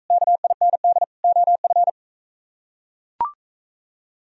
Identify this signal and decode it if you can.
Morse code, heard as a keyed tone, about 43 wpm, 700 Hz. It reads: KIND OF